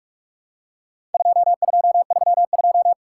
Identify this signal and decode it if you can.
Morse code, heard as a keyed tone, about 46 wpm, 700 Hz. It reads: 2232
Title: Worked All Europe DX Contest - QTC Exchange - 45wpm